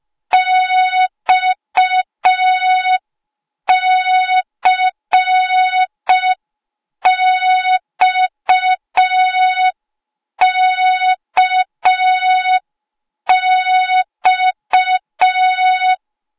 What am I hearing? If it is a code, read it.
Morse code, heard as a keyed tone, about 5 wpm, 3700 Hz. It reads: XCXKX
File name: Morze-HA2MN / hang / 08KXC HA2MN